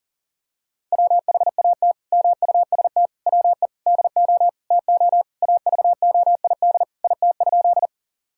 Morse code, heard as a keyed tone, about 40 wpm, 700 Hz. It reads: WHAT MUST WE DO TO AVOID IT?
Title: Sentences from Top 1000 Words - Review Part 6 - 40wpm